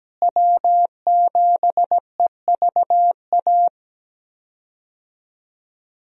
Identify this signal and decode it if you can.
Morse code, heard as a keyed tone, about 17 wpm, 700 Hz. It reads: W7EVA
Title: USA Call Signs - 5 Characters - Repeated 3x - 17wpm